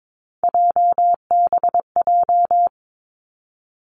Morse code, heard as a keyed tone, about 22 wpm, 700 Hz. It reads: JBJ
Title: Three Challenging Letter-Number Combinations 22wpm